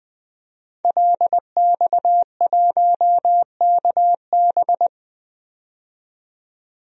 Morse code, heard as a keyed tone, about 20 wpm, 700 Hz. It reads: LX1KB
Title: DXCC-100 Call Signs - 5 Characters - 20wpm